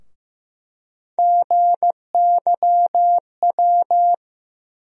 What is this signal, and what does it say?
Morse code, heard as a keyed tone, about 16 wpm, 700 Hz. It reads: GYW
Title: USA Call Sign Postfixes - 3 Letters 15wpm